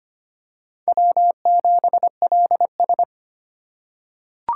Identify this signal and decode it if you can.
Morse code, heard as a keyed tone, about 25 wpm, 700 Hz. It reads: W7LS
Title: USA Call Signs - 4 Characters - 4x Character Spacing - 25wpm